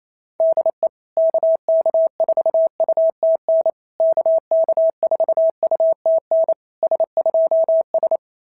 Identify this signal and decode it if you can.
Morse code, heard as a keyed tone, about 28 wpm, 690 Hz. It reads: DE KK4UTN KK4UTN S2S
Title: SOTA QSO Practice 28wpm